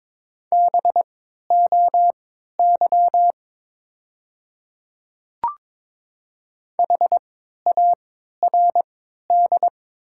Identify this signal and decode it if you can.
Morse code, heard as a keyed tone, about 22 wpm, spaced as for 12 wpm, 700 Hz. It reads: BOY HARD
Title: 3x Character Spacing - Top 300 Words - 22wpm